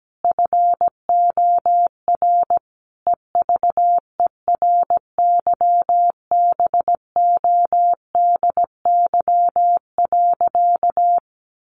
Morse code, heard as a keyed tone, about 17 wpm, 700 Hz. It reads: FOR EVERYBODY.
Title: Sentences from Top 1400 Words 17wpm